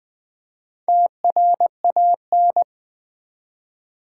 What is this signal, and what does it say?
Morse code, heard as a keyed tone, about 20 wpm, 700 Hz. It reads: TRAN
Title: Build A Word - 4 Letters At A Time - 8-Letter Words - 20wpm